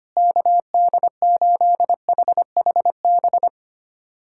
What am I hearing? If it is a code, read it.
Morse code, heard as a keyed tone, about 25 wpm, 700 Hz. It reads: KD8HHB
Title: USA Callsign DE Callsign - 6 Characters - 25wpm